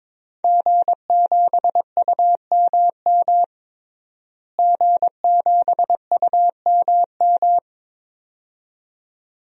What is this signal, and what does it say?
Morse code, heard as a keyed tone, about 22 wpm, 700 Hz. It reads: G7UMM G7UMM
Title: DXCC-100 Call Signs - 5 Characters - Repeated 2x - 22wpm